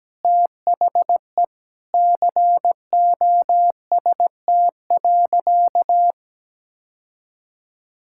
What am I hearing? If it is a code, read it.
Morse code, heard as a keyed tone, about 17 wpm, 700 Hz. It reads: THE COST.
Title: Sentences from Top 1000 Words - Review Part 5 - 17wpm